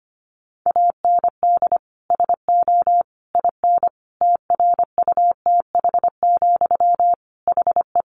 Morse code, heard as a keyed tone, about 25 wpm, 700 Hz. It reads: AND SO IN TRUTH, HE
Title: Sentences from Top 800 Words 25wpm